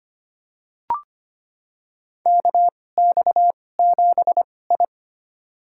Morse code, heard as a keyed tone, about 25 wpm, spaced as for 18 wpm, 700 Hz. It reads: KX7I
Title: USA Call Signs - 4 Characters - 2x Character Spacing - 25wpm